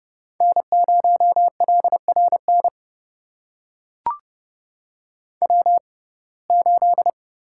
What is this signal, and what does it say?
Morse code, heard as a keyed tone, about 30 wpm, spaced as for 22 wpm, 700 Hz. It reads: N0LRN W8
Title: USA Call Signs - 5 Characters - 6x Character Spacing - 30wpm